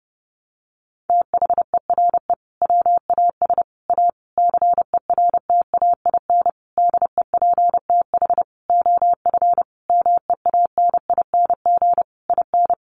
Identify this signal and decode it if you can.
Morse code, heard as a keyed tone, about 30 wpm, 700 Hz. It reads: THERE WAS A CERTAIN DEPTH OF MEANING IN
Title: Sentences from Top 1200 Words 30wpm